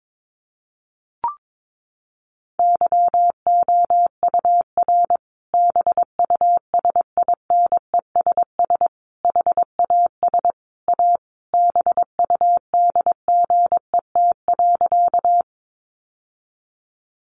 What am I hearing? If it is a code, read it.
Morse code, heard as a keyed tone, about 22 wpm, 700 Hz. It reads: YOUR BUSINESS HAS A BUDGET.
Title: Sentences from Top 1200 Words 22wpm